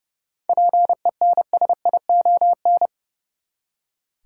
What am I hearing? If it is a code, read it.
Morse code, heard as a keyed tone, about 30 wpm, 700 Hz. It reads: PENSION
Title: Seven Letter Words - 30wpm